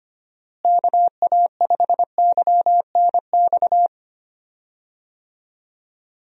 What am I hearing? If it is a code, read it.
Morse code, heard as a keyed tone, about 25 wpm, 700 Hz. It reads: KA5YNX
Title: USA Call Signs - 6 Characters - Repeated 2x - 25wpm